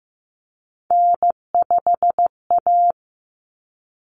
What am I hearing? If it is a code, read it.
Morse code, heard as a keyed tone, about 15 wpm, 700 Hz. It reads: N5A
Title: Common Cut Numbers - 3-Digit 15wpm